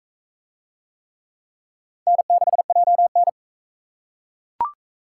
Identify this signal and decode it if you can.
Morse code, heard as a keyed tone, about 41 wpm, 700 Hz. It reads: N6JN